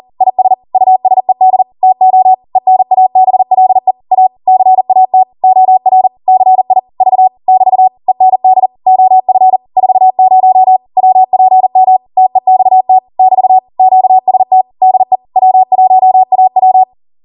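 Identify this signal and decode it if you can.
Morse code, heard as a keyed tone, about 40 wpm, 750 Hz. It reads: IS USED TO ENABLE A CAT OR CI V <BT> END OF 40 WPM TEXT <BT> QST DE W1AW
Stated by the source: Code practice at 40 wpm.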